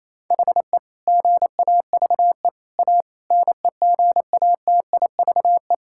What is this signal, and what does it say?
Morse code, heard as a keyed tone, about 28 wpm, 700 Hz. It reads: HE GAVE A NEGATIVE
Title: Sentences from Top 1000 Words - Review Part 6 - 28wpm